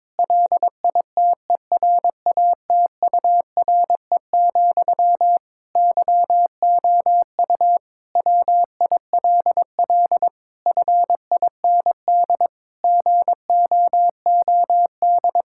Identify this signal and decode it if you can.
Morse code, heard as a keyed tone, about 22 wpm, 700 Hz. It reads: LITERATURE, YOU WILL FIND GOOD
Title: Sentences from Top 1000 Words - Review Part 6 - 22wpm